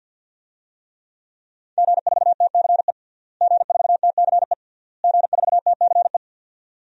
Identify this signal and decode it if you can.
Morse code, heard as a keyed tone, about 52 wpm, 700 Hz. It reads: G4TCE G4TCE G4TCE
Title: DXCC-100 Call Signs - 5 Characters - Repeated 3x - 50wpm